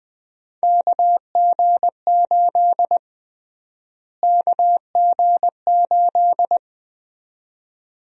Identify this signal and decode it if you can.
Morse code, heard as a keyed tone, about 20 wpm, 700 Hz. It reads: KG8 KG8
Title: USA Call Sign Prefixes - Repeated 2x - 20wpm